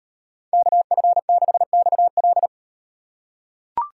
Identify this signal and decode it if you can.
Morse code, heard as a keyed tone, about 36 wpm, 700 Hz. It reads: KF6XL